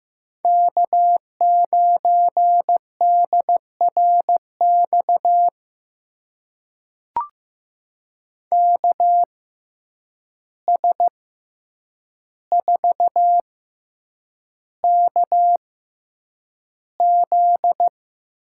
Morse code, heard as a keyed tone, about 15 wpm, spaced as for 8 wpm, 700 Hz. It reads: K9DRX KS4KZ
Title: USA Call Signs - 6x Character Spacing - 15wpm